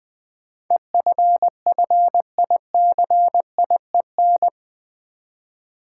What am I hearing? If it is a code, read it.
Morse code, heard as a keyed tone, about 20 wpm, 700 Hz. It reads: EFFICIEN
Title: Build A Word - 4 Letters At A Time - 10-Letter Words - 20wpm